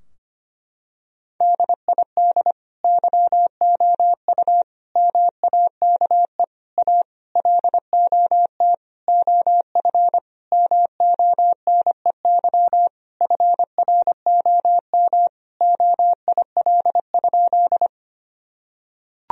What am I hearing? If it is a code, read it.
Morse code, heard as a keyed tone, about 25 wpm, 700 Hz. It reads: DID YOU MAKE A LOT OF MONEY FROM OIL?
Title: Sentences from Top 1000 Words - Review Part 4 - 25wpm